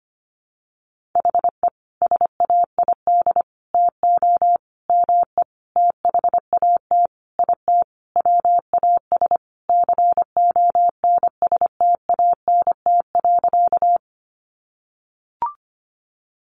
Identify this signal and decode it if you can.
Morse code, heard as a keyed tone, about 25 wpm, 700 Hz. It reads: HE SAID TO ME THAT IT WAS CONSTANT.
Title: Sentences from Top 1500 Words 25wpm